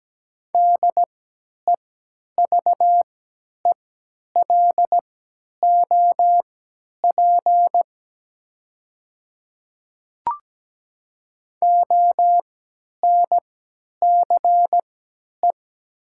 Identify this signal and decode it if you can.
Morse code, heard as a keyed tone, about 17 wpm, spaced as for 10 wpm, 700 Hz. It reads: DEVELOP ONCE
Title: 3x Character Spacing - Top 500 Words - 17wpm